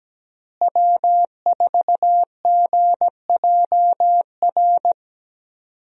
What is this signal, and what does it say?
Morse code, heard as a keyed tone, about 17 wpm, 700 Hz. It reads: W4GJR
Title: USA Call Signs - 5 Characters - Repeated 3x - 17wpm